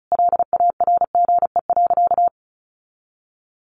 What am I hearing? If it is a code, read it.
Morse code, heard as a keyed tone, about 34 wpm, 700 Hz. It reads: LARGE.
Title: Sentences from Top 700 Words 35wpm